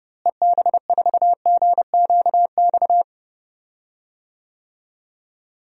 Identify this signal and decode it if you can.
Morse code, heard as a keyed tone, about 30 wpm, 700 Hz. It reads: EB4GQX
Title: DXCC-100 Call Signs - 6 Characters - Repeated 2x - 30wpm